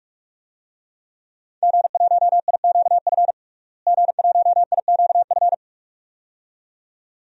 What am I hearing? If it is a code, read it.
Morse code, heard as a keyed tone, about 46 wpm, 700 Hz. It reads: G1IQF G1IQF
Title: DXCC-100 Call Signs - 5 Characters - Repeated 2x - 45wpm